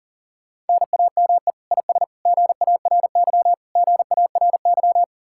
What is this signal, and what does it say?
Morse code, heard as a keyed tone, about 53 wpm, 700 Hz. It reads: NAME IS GARY GARY